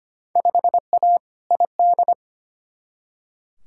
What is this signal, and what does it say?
Morse code, heard as a keyed tone, about 25 wpm, 700 Hz. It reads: 5A ID
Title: Field Day Exchanges Encore - 25wpm